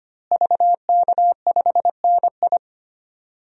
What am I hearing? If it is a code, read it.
Morse code, heard as a keyed tone, about 25 wpm, 700 Hz. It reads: VK5NI
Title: DXCC-100 Call Signs - 5 Characters - Repeated 2x - 25wpm